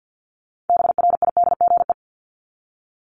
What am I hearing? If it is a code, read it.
Morse code, heard as a keyed tone, about 49 wpm, 700 Hz. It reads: BRIDGE
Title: Top 1001-1250 Words 50wpm